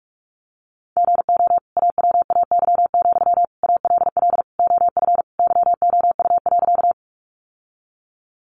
Morse code, heard as a keyed tone, about 44 wpm, 700 Hz. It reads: GO AWAY, ALL OF YOU.
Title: Sentences from Top 600 Words 45wpm